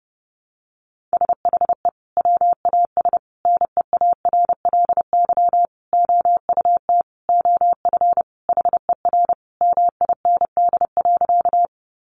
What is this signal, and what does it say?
Morse code, heard as a keyed tone, about 30 wpm, 700 Hz. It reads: SHE WAS NEARLY OUT OF HER MIND.
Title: Sentences from Top 1300 Words 30wpm